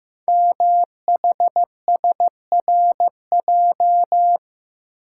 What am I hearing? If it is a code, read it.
Morse code, heard as a keyed tone, about 15 wpm, 700 Hz. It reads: MHSRJ
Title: Code Groups - 5 Letters - 15wpm